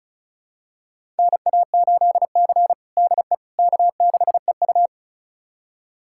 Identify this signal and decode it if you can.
Morse code, heard as a keyed tone, about 36 wpm, 700 Hz. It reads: NA8C DE K6EU
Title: USA Callsign DE Callsign - 4 Characters - 35wpm